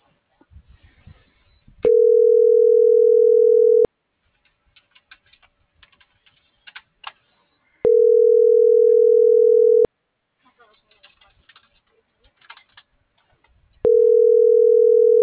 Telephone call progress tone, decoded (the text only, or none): ringback tone